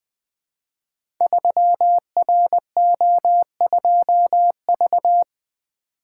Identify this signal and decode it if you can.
Morse code, heard as a keyed tone, about 20 wpm, 700 Hz. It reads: 3RO2V
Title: Code Groups - 5 Letter:Numbers - 20wpm